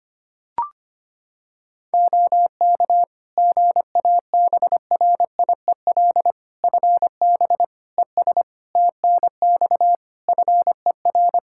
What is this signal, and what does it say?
Morse code, heard as a keyed tone, about 25 wpm, 700 Hz. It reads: OK GABRIEL FB ES TNX FER
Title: Practice QSO - Exchange 2 - Info - 25wpm